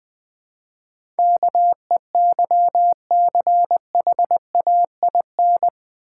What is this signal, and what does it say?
Morse code, heard as a keyed tone, about 20 wpm, 690 Hz. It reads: KEYCHAIN
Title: Difficult Words with Hints - 20wpm